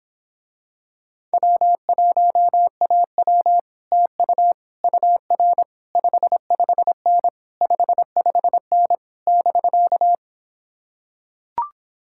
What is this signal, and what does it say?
Morse code, heard as a keyed tone, about 26 wpm, 700 Hz. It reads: W1AW TU UR 55N 55N <BK>